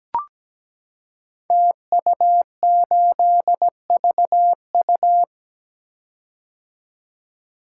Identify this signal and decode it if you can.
Morse code, heard as a keyed tone, about 17 wpm, 700 Hz. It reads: TU8VU
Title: Code Groups - 5 Letter:Numbers - 17wpm